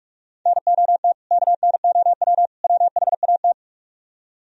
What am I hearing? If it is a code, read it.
Morse code, heard as a keyed tone, about 45 wpm, 700 Hz. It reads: NOT KNOW WHAT